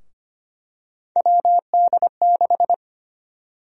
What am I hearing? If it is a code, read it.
Morse code, heard as a keyed tone, about 25 wpm, 700 Hz. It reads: WD6
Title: USA Call Sign Prefixes - 25wpm